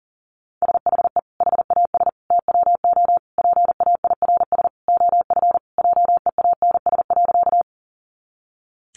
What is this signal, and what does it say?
Morse code, heard as a keyed tone, about 40 wpm, 700 Hz. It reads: SHE HAS TWO PAIRS OF JEANS.
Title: Additional Easy Phrases 40wpm